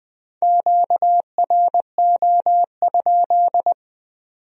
Morse code, heard as a keyed tone, about 20 wpm, 700 Hz. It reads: QRO?